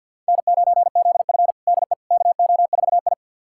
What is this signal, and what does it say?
Morse code, heard as a keyed tone, about 49 wpm, 700 Hz. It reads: N9ZF DE KO4I
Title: USA Callsign DE Callsign - 4 Characters - 50wpm